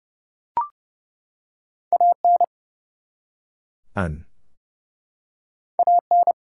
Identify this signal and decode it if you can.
Morse code, heard as a keyed tone, about 30 wpm, 700 Hz. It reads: AN AN
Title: Two Letter Words - 30wpm